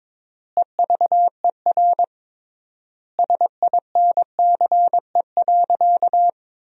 Morse code, heard as a keyed tone, about 22 wpm, 700 Hz. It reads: EVER SINCE.
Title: 3x Word-Spacing - Sentences from Top 400 Words - 22wpm